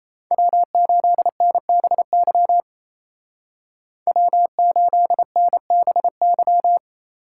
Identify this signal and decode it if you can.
Morse code, heard as a keyed tone, about 35 wpm, 700 Hz. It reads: W8NBY W8NBY